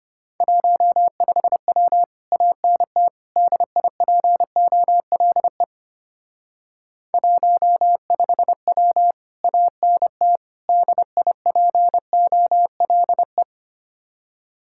Morse code, heard as a keyed tone, about 31 wpm, 700 Hz. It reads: 15W ANT DIPOLE 15W ANT DIPOLE
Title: Speed-Racing Common QSO Phrases - 25wpm